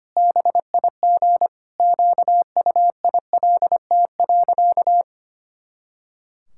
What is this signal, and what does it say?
Morse code, heard as a keyed tone, about 25 wpm, 700 Hz. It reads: BIG QUILT.
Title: Pangram Challenge v2 - 25wpm